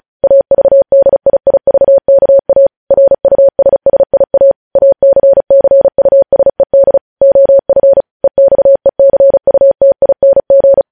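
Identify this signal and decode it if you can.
Morse code, heard as a keyed tone, about 36 wpm, 550 Hz. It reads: AVDIIVKA RUSSIA ACCUSED OF EXECUTING
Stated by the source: Morse code news headlines sent at 35, beginning with vvv.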